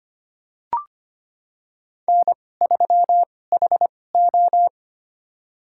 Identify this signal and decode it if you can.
Morse code, heard as a keyed tone, about 25 wpm, spaced as for 18 wpm, 700 Hz. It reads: N3HO
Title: USA Call Signs - 4 Characters - 2x Character Spacing - 25wpm